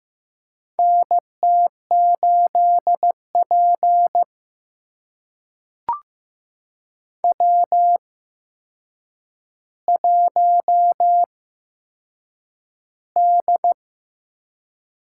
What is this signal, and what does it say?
Morse code, heard as a keyed tone, about 15 wpm, spaced as for 7 wpm, 700 Hz. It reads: NT8P W1D
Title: USA Call Signs - 8x Character Spacing - 15wpm